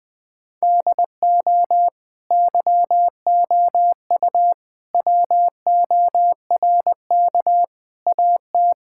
Morse code, heard as a keyed tone, about 20 wpm, 700 Hz. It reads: DO YOU WORK AT
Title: Sentences from Top 1000 Words - Review Part 5 - 20wpm